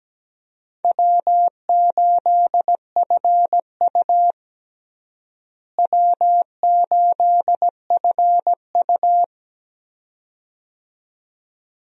Morse code, heard as a keyed tone, about 17 wpm, 700 Hz. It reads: W8FU W8FU
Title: USA Call Signs - 4 Characters - Repeated 2x - 17wpm